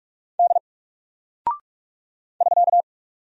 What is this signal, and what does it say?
Morse code, heard as a keyed tone, about 44 wpm, 700 Hz. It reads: D <SK>
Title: Single Character (Letter:Number:Punctuation:Prosign) 45wpm